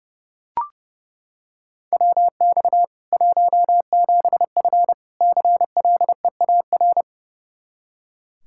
Morse code, heard as a keyed tone, about 30 wpm, 700 Hz. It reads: WX 17F CLEAR
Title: Practice QSO - Exchange 2 - Info - 30wpm